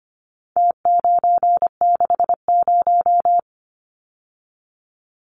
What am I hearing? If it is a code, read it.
Morse code, heard as a keyed tone, about 25 wpm, 700 Hz. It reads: T960
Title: Numbers - Contest Style - 4-Digit 25wpm